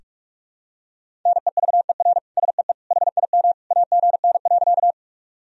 Morse code, heard as a keyed tone, about 46 wpm, 700 Hz. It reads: NEVER SEE HIM AGN.
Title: Sentences from Top 300 Words and Abbreviations 45wpm